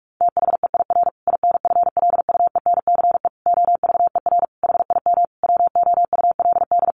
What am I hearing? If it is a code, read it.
Morse code, heard as a keyed tone, about 44 wpm, 700 Hz. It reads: THEIR INFLUENCE OVER HIM WOULD
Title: Sentences from Top 1400 Words 45wpm